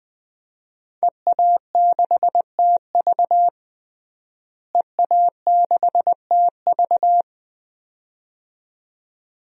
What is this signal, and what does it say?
Morse code, heard as a keyed tone, about 20 wpm, 700 Hz. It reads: EA6TV EA6TV
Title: DXCC-100 Call Signs - 5 Characters - Repeated 2x - 20wpm